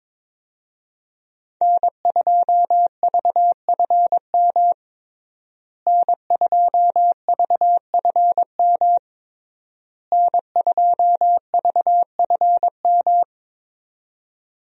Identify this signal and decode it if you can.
Morse code, heard as a keyed tone, about 22 wpm, 700 Hz. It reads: N2VFM N2VFM N2VFM
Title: USA Call Signs - 5 Characters - Repeated 3x - 22wpm